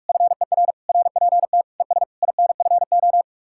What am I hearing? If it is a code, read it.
Morse code, heard as a keyed tone, about 64 wpm, spaced as for 54 wpm, 700 Hz. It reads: FER RPT ES INFO
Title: Speed-Racing Common QSO Phrases - 45wpm